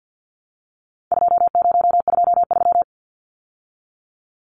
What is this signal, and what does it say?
Morse code, heard as a keyed tone, about 49 wpm, 700 Hz. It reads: 2023
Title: Numbers - 4-Digit 50wpm